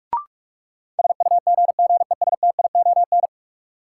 Morse code, heard as a keyed tone, about 46 wpm, 700 Hz. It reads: SUGGESTION
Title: Sentences from Top 1600 Words - New Word Drill Alt - 45wpm